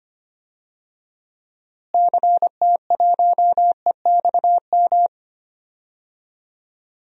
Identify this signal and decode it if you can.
Morse code, heard as a keyed tone, about 25 wpm, 700 Hz. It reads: CT1EXM
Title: DXCC-100 Call Signs - 6 Characters - Repeated 2x - 25wpm